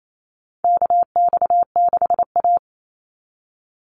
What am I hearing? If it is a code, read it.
Morse code, heard as a keyed tone, about 28 wpm, 700 Hz. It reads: KX6A
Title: NA SOTA Chaser Callsigns 28wpm